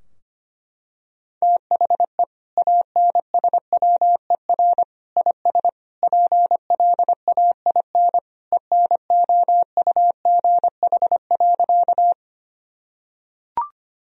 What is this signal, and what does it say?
Morse code, heard as a keyed tone, about 25 wpm, 700 Hz. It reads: THE ANSWER IS PLAIN ENOUGH.
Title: Sentences from Top 1000 Words - Review Part 2 - 25wpm